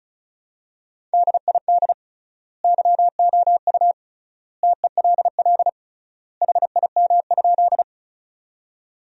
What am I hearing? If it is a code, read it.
Morse code, heard as a keyed tone, about 36 wpm, 700 Hz. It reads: DID YOU TELL HIM?
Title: 3x Word-Spacing - Sentences from Top 200 Words - 35wpm